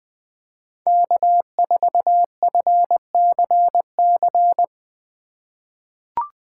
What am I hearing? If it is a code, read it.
Morse code, heard as a keyed tone, about 20 wpm, 700 Hz. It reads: K4FCC
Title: K1USN SST QSO Practice - 20wpm